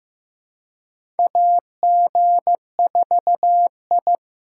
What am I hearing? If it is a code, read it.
Morse code, heard as a keyed tone, about 15 wpm, 700 Hz. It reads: AG4I
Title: USA Call Signs - 4 Characters - 2x Character Spacing - 15wpm